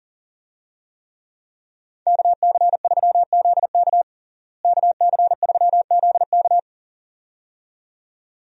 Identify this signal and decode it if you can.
Morse code, heard as a keyed tone, about 40 wpm, 700 Hz. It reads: KC3ZK KC3ZK
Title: USA Call Signs - 5 Characters - Repeated 2x - 40wpm